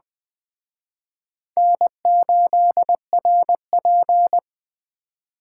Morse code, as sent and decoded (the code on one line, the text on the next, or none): -. ---.. .-. .--.
N8RP